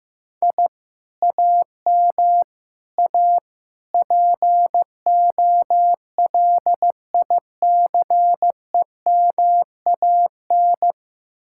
Morse code, heard as a keyed tone, about 15 wpm, 700 Hz. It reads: I AM A POLICEMAN
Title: Speed-Racing Common QSO Phrases Encore - 15wpm